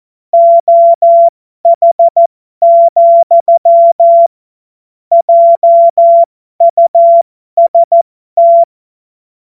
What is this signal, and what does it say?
Morse code, heard as a keyed tone, about 14 wpm, spaced as for 12 wpm, 680 Hz. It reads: OH, JUST
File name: Morsekurs / Witze